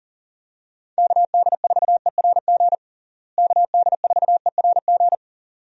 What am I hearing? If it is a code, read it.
Morse code, heard as a keyed tone, about 40 wpm, 700 Hz. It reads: KD4ERG KD4ERG
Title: USA Call Signs - 6 Characters - Repeated 2x - 40wpm